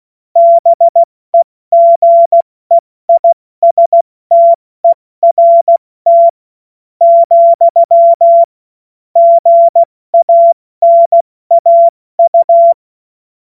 Morse code, heard as a keyed tone, about 16 wpm, spaced as for 14 wpm, 680 Hz. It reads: BEGEISTERT, GANAU
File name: Morsekurs / Witze / Witz 17.Oktober 2023 70 F80